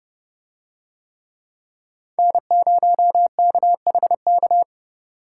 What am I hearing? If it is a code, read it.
Morse code, heard as a keyed tone, about 30 wpm, 700 Hz. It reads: N0KHK